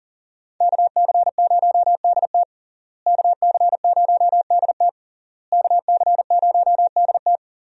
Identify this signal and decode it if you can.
Morse code, heard as a keyed tone, about 40 wpm, 700 Hz. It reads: KC0DT KC0DT KC0DT
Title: USA Call Signs - 5 Characters - Repeated 3x - 40wpm